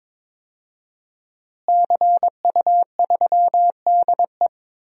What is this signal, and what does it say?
Morse code, heard as a keyed tone, about 22 wpm, 700 Hz. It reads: CU3DE
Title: DXCC-100 Call Signs - 5 Characters - Repeated 3x - 22wpm